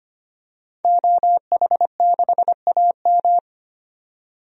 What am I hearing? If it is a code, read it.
Morse code, heard as a keyed tone, about 25 wpm, 700 Hz. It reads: OH6AM
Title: DXCC-100 Call Signs - 5 Characters - 25wpm